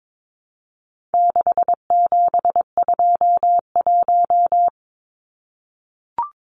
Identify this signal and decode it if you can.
Morse code, heard as a keyed tone, about 22 wpm, 700 Hz. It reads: 6721
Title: Numbers - 4-Digit 22wpm